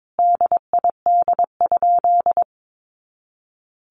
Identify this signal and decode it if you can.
Morse code, heard as a keyed tone, about 22 wpm, 700 Hz. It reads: DID?
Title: Sentences from Top 100 Words 22wpm